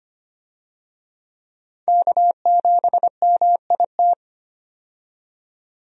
Morse code, heard as a keyed tone, about 25 wpm, 700 Hz. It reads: K7MIT
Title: USA Call Signs - 5 Characters - 25wpm